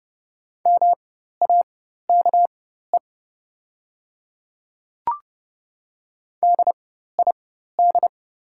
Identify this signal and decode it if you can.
Morse code, heard as a keyed tone, about 30 wpm, spaced as for 14 wpm, 700 Hz. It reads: MAKE DID
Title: 4x Character Spacing - Top 200 Words - 30wpm